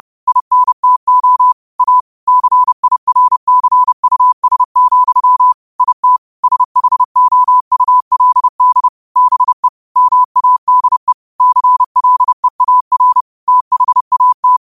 Morse code, heard as a keyed tone, about 30 wpm, 1000 Hz. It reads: INTO A CIRCUS, IT SHOULD BE MADE CLEAR THAT
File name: qotdfiles / 30wpm